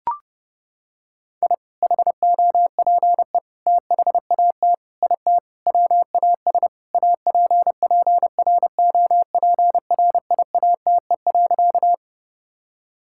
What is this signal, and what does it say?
Morse code, heard as a keyed tone, about 30 wpm, 700 Hz. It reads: I HOPE THAT IT WAS APPROPRIATE.
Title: Sentences from Top 1000 Words - Review Part 5 - 30wpm